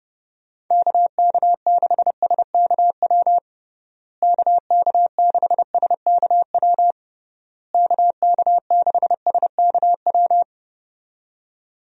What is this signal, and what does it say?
Morse code, heard as a keyed tone, about 32 wpm, 700 Hz. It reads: KK6SKW KK6SKW KK6SKW
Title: USA Call Signs - 6 Characters - Repeated 3x - 30wpm